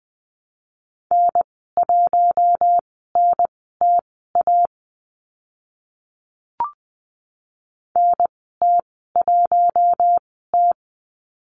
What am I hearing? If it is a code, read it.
Morse code, heard as a keyed tone, about 20 wpm, spaced as for 15 wpm, 700 Hz. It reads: N1NTA NT1T
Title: 042 Lesson 8g - Number 1 - Call Signs 20wpm